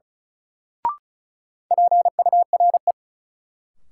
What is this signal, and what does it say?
Morse code, heard as a keyed tone, about 36 wpm, 700 Hz. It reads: PURE
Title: Sentences from Top 1500 Words - New Word Drill Alt - 35wpm